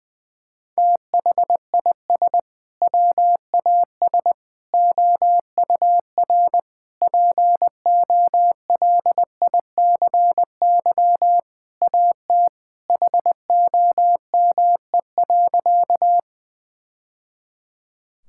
Morse code, heard as a keyed tone, about 20 wpm, 700 Hz. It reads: THIS WAS OUR POLICY AT HOME.
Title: Sentences from Top 1000 Words - Review Part 5 - 20wpm